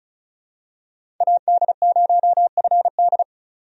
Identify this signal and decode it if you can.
Morse code, heard as a keyed tone, about 36 wpm, 700 Hz. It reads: AD0FD
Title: USA Call Signs - 5 Characters - 35wpm